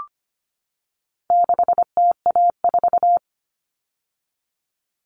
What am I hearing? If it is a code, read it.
Morse code, heard as a keyed tone, about 25 wpm, 700 Hz. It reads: 6TA4
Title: All Cut Numbers - 4-Digit 25wpm